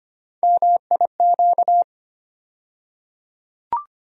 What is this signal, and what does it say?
Morse code, heard as a keyed tone, about 25 wpm, 700 Hz. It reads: MIQ
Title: USA Call Sign Postfixes - 3 Letters 25wpm